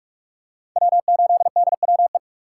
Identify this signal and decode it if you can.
Morse code, heard as a keyed tone, about 44 wpm, 700 Hz. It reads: W8DWE
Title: USA Call Signs - 5 Characters - 2x Character Spacing - 45wpm